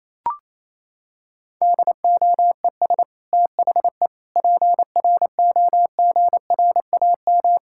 Morse code, heard as a keyed tone, about 28 wpm, 700 Hz. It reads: DOES THE PROGRAM
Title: Sentences from Top 1000 Words - Review Part 5 - 28wpm